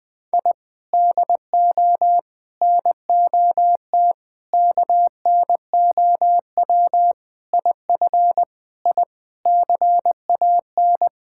Morse code, heard as a keyed tone, about 20 wpm, 700 Hz. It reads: I DO NOT KNOW IF I CAN
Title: Sentences with 5-Word Phrases - 20wpm